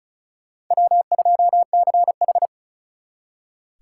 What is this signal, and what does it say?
Morse code, heard as a keyed tone, about 43 wpm, 700 Hz. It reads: W2CH
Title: Speed Racing - USA 1x2 Call Signs 35wpm